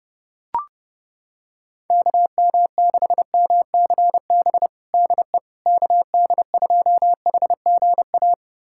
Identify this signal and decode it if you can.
Morse code, heard as a keyed tone, about 30 wpm, 700 Hz. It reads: KM6MCB DE KD2HGA
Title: USA Callsign DE Callsign 30wpm